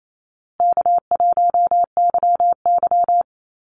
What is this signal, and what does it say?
Morse code, heard as a keyed tone, about 28 wpm, 700 Hz. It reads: K1YY
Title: Easy Call Signs Repeated 3x - 28wpm